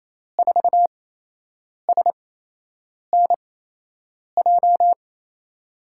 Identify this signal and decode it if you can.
Morse code, heard as a keyed tone, about 28 wpm, spaced as for 8 wpm, 700 Hz. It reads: 4SNJ
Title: USA Call Signs - 6 Characters - 8x Character Spacing - 28wpm